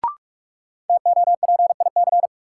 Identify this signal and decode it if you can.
Morse code, heard as a keyed tone, about 42 wpm, 700 Hz. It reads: TOPIC
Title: Sentences from Top 1200 Words - New Word Drill Alt - 45wpm